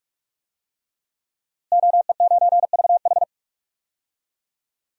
Morse code, heard as a keyed tone, about 46 wpm, 700 Hz. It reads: OE9VH
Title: DXCC-100 Call Signs - 5 Characters - Repeated 3x - 45wpm